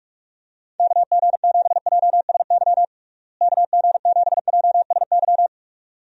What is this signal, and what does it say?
Morse code, heard as a keyed tone, about 46 wpm, 700 Hz. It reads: KG7JSY KG7JSY
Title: USA Call Signs - 6 Characters - Repeated 2x - 45wpm